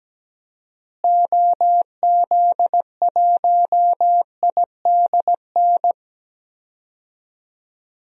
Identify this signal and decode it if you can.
Morse code, heard as a keyed tone, about 17 wpm, 700 Hz. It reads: OZ1IDN
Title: DXCC-100 Call Signs - 6 Characters - Repeated 3x - 17wpm